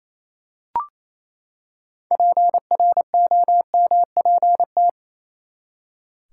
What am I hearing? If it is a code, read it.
Morse code, heard as a keyed tone, about 28 wpm, 700 Hz. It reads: PROMPT
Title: Six Letter Words - 28wpm